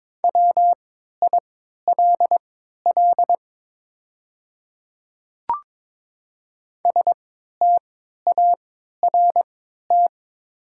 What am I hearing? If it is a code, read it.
Morse code, heard as a keyed tone, about 22 wpm, spaced as for 12 wpm, 700 Hz. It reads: WILL START
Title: 3x Character Spacing - Top 300 Words - 22wpm